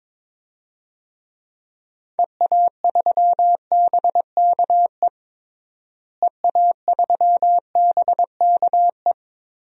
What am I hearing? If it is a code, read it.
Morse code, heard as a keyed tone, about 21 wpm, 700 Hz. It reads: EA3BKE EA3BKE
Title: DXCC-100 Call Signs - 6 Characters - Repeated 3x - 22wpm